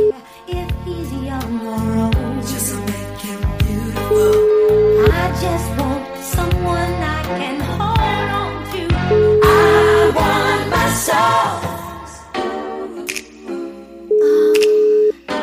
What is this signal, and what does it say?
Ringback tone (Europe, 425 Hz), with noise.